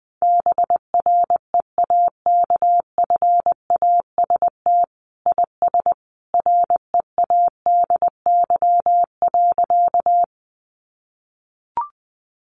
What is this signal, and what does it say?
Morse code, heard as a keyed tone, about 20 wpm, 700 Hz. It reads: BREAKFAST IS READY.
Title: Sentences from Top 1500 Words 20wpm